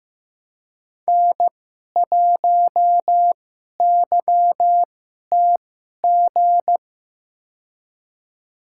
Morse code, heard as a keyed tone, about 15 wpm, spaced as for 11 wpm, 700 Hz. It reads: N1YTG
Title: USA Call Signs - 5 Characters - 2x Character Spacing - 15wpm